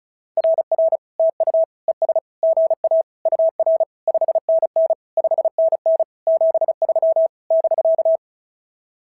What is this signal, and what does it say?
Morse code, heard as a keyed tone, about 33 wpm, 650 Hz. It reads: RR TU ES GA UR 5NN 5NN 73 <BK>
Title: SOTA QSO Practice 35wpm